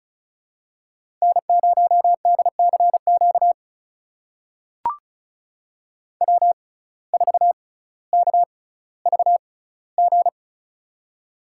Morse code, heard as a keyed tone, about 34 wpm, spaced as for 18 wpm, 700 Hz. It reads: N0DCQ W4KVG
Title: USA Call Signs - 5 Characters - 6x Character Spacing - 35wpm